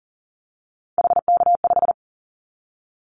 Morse code, heard as a keyed tone, about 40 wpm, 700 Hz. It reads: HK5
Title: Callsign Prefixes for Top 100 DXCC - 40wpm